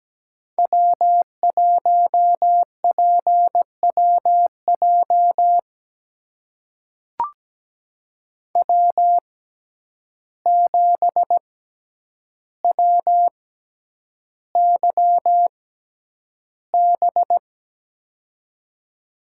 Morse code, heard as a keyed tone, about 17 wpm, spaced as for 9 wpm, 700 Hz. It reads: W1PWJ W7WYB
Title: USA Call Signs - 5 Characters - 6x Character Spacing - 17wpm